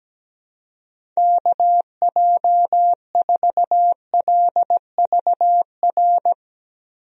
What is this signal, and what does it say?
Morse code, heard as a keyed tone, about 17 wpm, 700 Hz. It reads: KJ4LVR
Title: USA Call Signs - 6 Characters - 17wpm